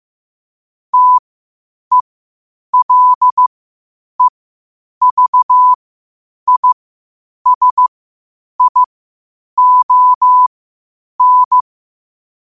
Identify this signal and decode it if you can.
Morse code, heard as a keyed tone, about 15 wpm, spaced as for 8 wpm, 1000 Hz. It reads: TELEVISION
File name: qotdfiles / 8wpm